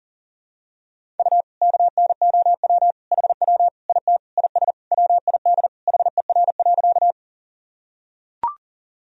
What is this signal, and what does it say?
Morse code, heard as a keyed tone, about 40 wpm, 700 Hz. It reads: U KNOW HW IT IS WID HER.
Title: Sentences from Top 100 Words and Abbreviations 40wpm